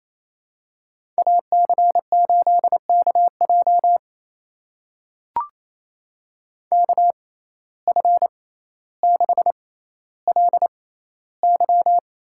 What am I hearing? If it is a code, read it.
Morse code, heard as a keyed tone, about 28 wpm, spaced as for 14 wpm, 700 Hz. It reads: AC8KJ KF6LY
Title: USA Call Signs - 5 Characters - 6x Character Spacing - 28wpm